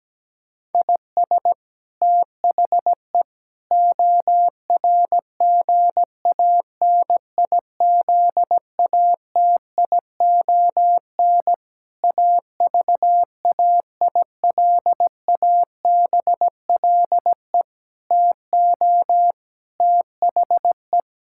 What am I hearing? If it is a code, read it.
Morse code, heard as a keyed tone, about 17 wpm, 700 Hz. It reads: IS THE ORGANIZATION AVAILABLE TO THE
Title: Sentences from Top 1000 Words - Review Part 5 - 17wpm